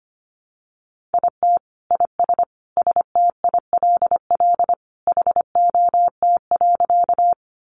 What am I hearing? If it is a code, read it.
Morse code, heard as a keyed tone, about 25 wpm, 700 Hz. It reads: IT IS STILL HOT.
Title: Additional Easy Phrases 25wpm